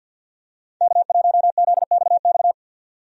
Morse code, heard as a keyed tone, about 52 wpm, 700 Hz. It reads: K1ZXX
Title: USA Call Signs - 5 Characters - 50wpm